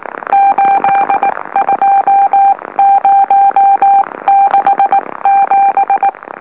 Morse code, heard as a keyed tone, about 19 wpm, spaced as for 16 wpm, 790 Hz. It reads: 82067
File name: M04- -2345utc-LOLO